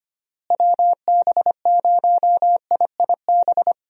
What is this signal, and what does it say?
Morse code, heard as a keyed tone, about 25 wpm, 700 Hz. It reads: WB0IIB
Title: USA Call Signs - 6 Characters - Repeated 3x - 25wpm